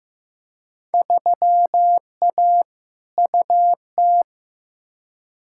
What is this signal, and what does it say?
Morse code, heard as a keyed tone, about 15 wpm, 700 Hz. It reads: 3A UT
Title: Field Day Exchanges - 15wpm